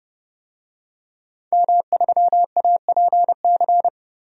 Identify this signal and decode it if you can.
Morse code, heard as a keyed tone, about 30 wpm, 700 Hz. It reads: M3APC
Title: DXCC-100 Call Signs - 5 Characters - Repeated 3x - 30wpm